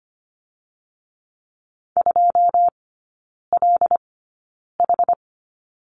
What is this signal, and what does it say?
Morse code, heard as a keyed tone, about 25 wpm, spaced as for 9 wpm, 700 Hz. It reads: 2LH
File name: Lesson 14e - Number 2 - Three Characters 25wpm